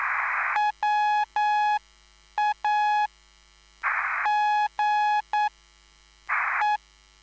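Morse code, heard as a keyed tone, about 9 wpm, spaced as for 6 wpm, 840 Hz. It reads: WAGE
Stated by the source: Live CW QSO between PA2BX and SP3JFK on the 14055 kHz QRS frequency, at a basic 8 wpm.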